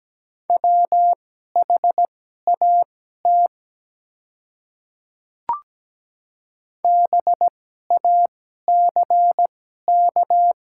Morse code, heard as a keyed tone, about 17 wpm, spaced as for 12 wpm, 700 Hz. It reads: WHAT BACK
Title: 2x Character Spacing - Top 200 Words - 17wpm